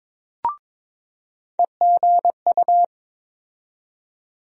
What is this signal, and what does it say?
Morse code, heard as a keyed tone, about 21 wpm, 700 Hz. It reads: EGU
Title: USA Call Sign Postfixes - 3 Letters 22wpm